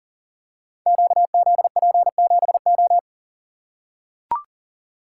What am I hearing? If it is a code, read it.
Morse code, heard as a keyed tone, about 42 wpm, 700 Hz. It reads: QZP7O
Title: Code Groups - 5 Letter:Numbers - 40wpm